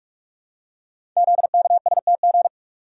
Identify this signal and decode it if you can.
Morse code, heard as a keyed tone, about 44 wpm, 700 Hz. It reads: ZKSTG
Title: Code Groups - 5 Letters - 45wpm